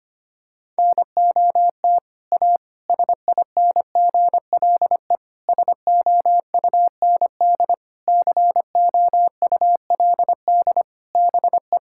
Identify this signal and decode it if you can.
Morse code, heard as a keyed tone, about 25 wpm, 700 Hz. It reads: NOT A SINGLE SOUND COULD BE
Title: Sentences from Top 1000 Words - Review Part 5 - 25wpm